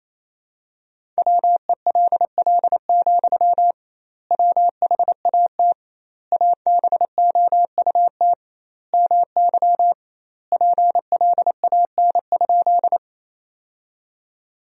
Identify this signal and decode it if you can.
Morse code, heard as a keyed tone, about 27 wpm, 700 Hz. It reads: WELL, WHAT ABOUT MY PLAN?
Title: 2x Word-Spacing - Sentences from Top 500 Words - 28wpm